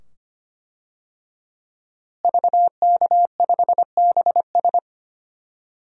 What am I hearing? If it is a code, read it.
Morse code, heard as a keyed tone, about 25 wpm, 700 Hz. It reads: VK5BS
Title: DXCC-100 Call Signs - 5 Characters - Repeated 2x - 25wpm